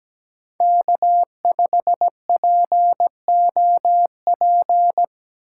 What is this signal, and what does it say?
Morse code, heard as a keyed tone, about 17 wpm, 700 Hz. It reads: K5POP
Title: USA Call Signs - 5 Characters - 6x Character Spacing - 17wpm